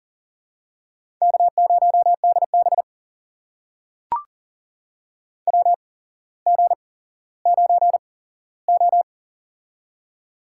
Morse code, heard as a keyed tone, about 38 wpm, spaced as for 17 wpm, 700 Hz. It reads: K0DB WG9O
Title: USA Call Signs - 4 Characters - 8x Character Spacing - 40wpm